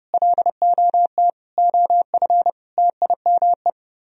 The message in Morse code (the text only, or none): LOT OF TIME